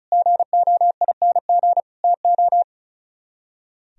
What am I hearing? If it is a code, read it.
Morse code, heard as a keyed tone, about 46 wpm, 700 Hz. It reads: GOING TO